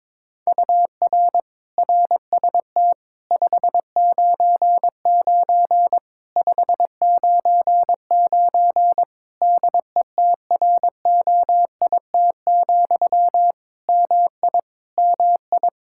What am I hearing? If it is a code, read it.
Morse code, heard as a keyed tone, about 22 wpm, 700 Hz. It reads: UR RST 599 599 DETROIT, MI MI
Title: Practice QSO - Exchange 1 - Beginning - 22wpm